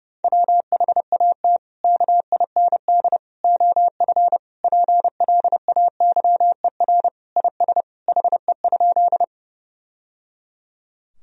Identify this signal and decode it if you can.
Morse code, heard as a keyed tone, about 31 wpm, 700 Hz. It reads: WHAT KIND OF PLAYER IS HE?
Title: Sentences from Top 1000 Words - Review Part 4 - 30wpm